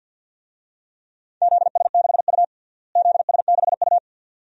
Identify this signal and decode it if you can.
Morse code, heard as a keyed tone, about 52 wpm, 700 Hz. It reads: ZS6U ZS6U
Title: DXCC-100 Call Signs - 4 Characters - Repeated 2x - 50wpm